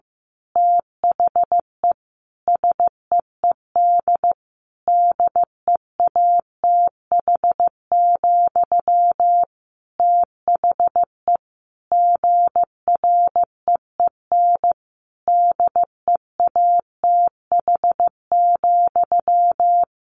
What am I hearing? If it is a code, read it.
Morse code, heard as a keyed tone, about 15 wpm, 700 Hz. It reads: THE SEED DEATH, THE GREEN DEATH,
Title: A Sound Of Thunder 15wpm